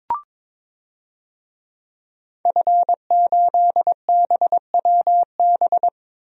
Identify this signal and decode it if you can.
Morse code, heard as a keyed tone, about 22 wpm, 700 Hz. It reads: F8BWB
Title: DXCC-100 Call Signs - 5 Characters - 22wpm